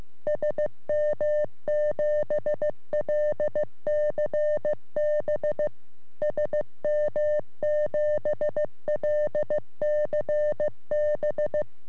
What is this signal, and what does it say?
Morse code, heard as a keyed tone, about 15 wpm, 600 Hz. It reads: SM7LCB SM7LCB
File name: sm7lcb / g3xdy 3cm